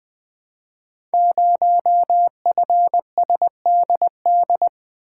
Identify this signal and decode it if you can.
Morse code, heard as a keyed tone, about 20 wpm, 700 Hz. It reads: 0FSDD
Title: Code Groups - 5 Letter:Numbers - 20wpm